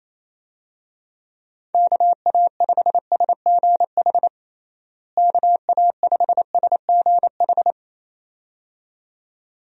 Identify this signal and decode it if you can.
Morse code, heard as a keyed tone, about 28 wpm, 700 Hz. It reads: KA5SGH KA5SGH
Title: USA Call Signs - 6 Characters - Repeated 2x - 28wpm